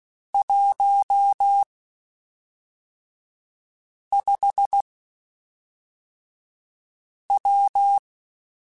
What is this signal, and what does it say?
Morse code, heard as a keyed tone, about 16 wpm, spaced as for 3 wpm, 800 Hz. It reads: 15W
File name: QSOs / QSOs-mp3s / WD0BGP 1@5